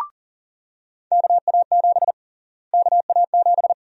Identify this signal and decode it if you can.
Morse code, heard as a keyed tone, about 40 wpm, 700 Hz. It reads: KA7 KA7
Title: USA Call Sign Prefixes - Repeated 2x - 40wpm